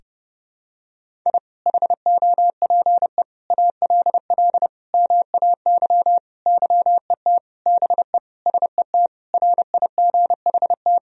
Morse code, heard as a keyed tone, about 32 wpm, 700 Hz. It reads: I HOPE ALL MAY YET BE SET RIGHT
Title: Sentences from Top 1000 Words - Review Part 1 - 30wpm